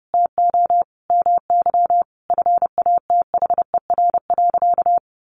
Morse code, heard as a keyed tone, about 30 wpm, 700 Hz. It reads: TO MY FATHER.
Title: Sentences from Top 1300 Words 30wpm